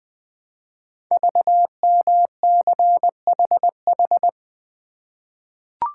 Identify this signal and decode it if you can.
Morse code, heard as a keyed tone, about 20 wpm, 700 Hz. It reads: VMCHH
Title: Code Groups - 5 Letter:Numbers - 20wpm